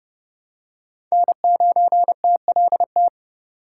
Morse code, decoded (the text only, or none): N9TLT